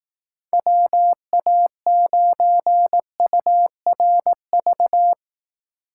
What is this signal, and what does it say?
Morse code, heard as a keyed tone, about 18 wpm, 700 Hz. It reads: WA9URV
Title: Speed Racing - USA 2x3 Call Signs 15wpm